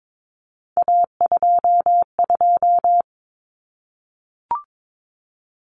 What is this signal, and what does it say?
Morse code, heard as a keyed tone, about 22 wpm, 700 Hz. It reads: A22
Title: Common Cut Numbers - 3-Digit 22wpm